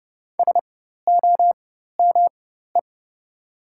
Morse code, heard as a keyed tone, about 31 wpm, spaced as for 14 wpm, 700 Hz. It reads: SOME
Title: 4x Character Spacing - Top 300 Words - 30wpm